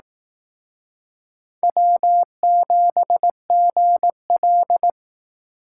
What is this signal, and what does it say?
Morse code, heard as a keyed tone, about 18 wpm, 700 Hz. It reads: W7GL
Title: Speed Racing - USA 1x2 Call Signs 15wpm